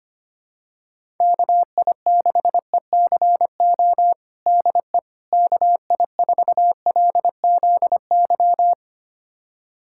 Morse code, heard as a keyed tone, about 25 wpm, 700 Hz. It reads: KI6ECO DE KI4LZY
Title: USA Callsign DE Callsign - 6 Characters - 25wpm